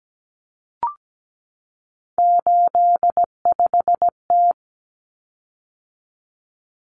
Morse code, heard as a keyed tone, about 17 wpm, 700 Hz. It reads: 85T
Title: Common Cut Numbers - 3-Digit 17wpm